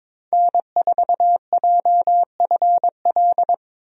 Morse code, heard as a keyed tone, about 22 wpm, 700 Hz. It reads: N4JFL
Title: USA Call Signs - 5 Characters - 22wpm